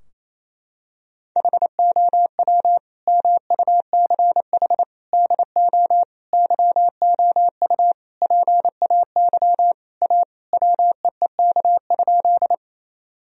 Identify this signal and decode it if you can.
Morse code, heard as a keyed tone, about 27 wpm, 700 Hz. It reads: HOW MUCH DO YOU PAY A WEEK?
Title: Sentences from Top 1000 Words - Review Part 2 - 28wpm